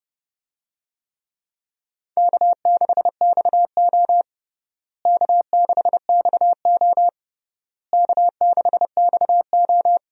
Morse code, heard as a keyed tone, about 32 wpm, 700 Hz. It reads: K6XO K6XO K6XO
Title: USA Call Signs - 4 Characters - Repeated 3x - 30wpm